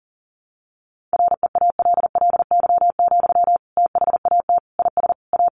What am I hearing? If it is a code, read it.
Morse code, heard as a keyed tone, about 40 wpm, 700 Hz. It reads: REALLY, THAT IS A
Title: Sentences from Top 600 Words 40wpm